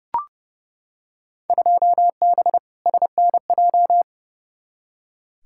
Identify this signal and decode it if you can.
Morse code, heard as a keyed tone, about 30 wpm, 700 Hz. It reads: 2B SNJ
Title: Field Day Exchanges Encore - 30wpm